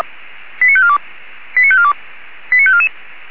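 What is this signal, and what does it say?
Five-tone selective call (ZVEI), heard as quick runs of five tones, tones of about 70 ms. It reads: 89542 89542 89544
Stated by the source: ZVEI five-tone selective call 89288 received via SDR, from the Feuerwehr Orlamünde (feuerwehr-orlamuende.org) audio server.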